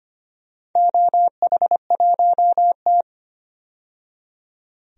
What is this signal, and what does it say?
Morse code, heard as a keyed tone, about 25 wpm, 700 Hz. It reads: OH1T
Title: DXCC-100 Call Signs - 4 Characters - 25wpm